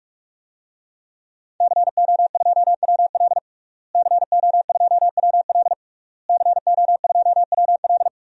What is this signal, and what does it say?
Morse code, heard as a keyed tone, about 45 wpm, 700 Hz. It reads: CO2WL CO2WL CO2WL
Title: DXCC-100 Call Signs - 5 Characters - Repeated 3x - 45wpm